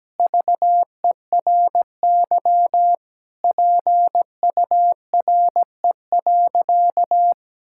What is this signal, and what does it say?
Morse code, heard as a keyed tone, about 17 wpm, 700 Hz. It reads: VERY PURE.
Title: Sentences from Top 1600 Words 17wpm